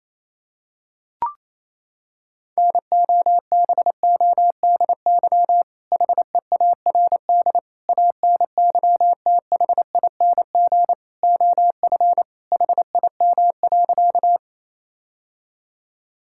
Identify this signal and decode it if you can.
Morse code, heard as a keyed tone, about 28 wpm, 700 Hz. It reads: NOBODY HEARD ANYTHING OF HIM.
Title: Sentences from Top 1600 Words 28wpm